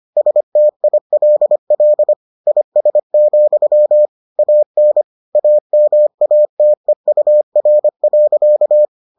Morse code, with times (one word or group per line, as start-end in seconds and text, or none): 0.2-2.1 STILL
2.5-4.1 IS,
4.4-5.0 AN
5.3-8.8 AMATEUR.